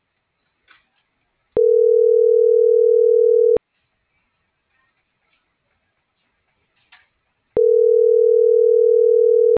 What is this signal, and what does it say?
Ringback tone (North America), 440 + 480 Hz.
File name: CENTRAL SKYBLUE / out lopez.brandon